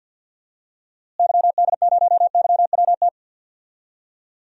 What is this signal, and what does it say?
Morse code, heard as a keyed tone, about 52 wpm, 700 Hz. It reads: YD0YWT
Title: DXCC-100 Call Signs - 6 Characters - Repeated 3x - 50wpm